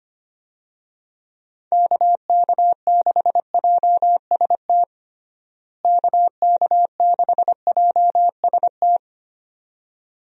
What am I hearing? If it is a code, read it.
Morse code, heard as a keyed tone, about 25 wpm, 700 Hz. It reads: KK6JST KK6JST
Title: USA Call Signs - 6 Characters - Repeated 2x - 25wpm